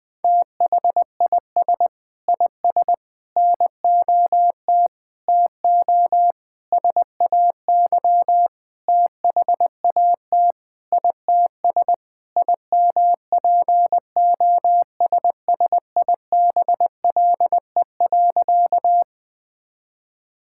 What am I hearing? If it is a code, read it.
Morse code, heard as a keyed tone, about 21 wpm, 700 Hz. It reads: THIS IS NOT TO SAY THAT ITS IMPOSSIBLE.
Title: Sentences with 5-Word Phrases - Spoken Beginning - 20wpm